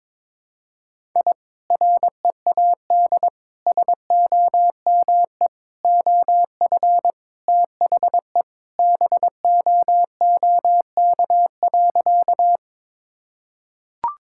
Morse code, heard as a keyed tone, about 22 wpm, 700 Hz. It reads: I READ SOME OF THE BOOK.
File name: practice / 22wpm / Sentences with 2-Word Phrases - 22wpm